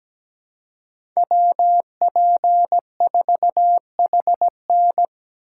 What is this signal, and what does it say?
Morse code, heard as a keyed tone, about 17 wpm, 700 Hz. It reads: WP4HN
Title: USA Call Signs - 5 Characters - 3x Character Spacing - 17wpm